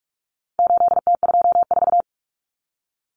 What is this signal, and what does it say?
Morse code, heard as a keyed tone, about 44 wpm, 700 Hz. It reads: 8T24
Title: Common Cut Numbers - 4-Digit 45wpm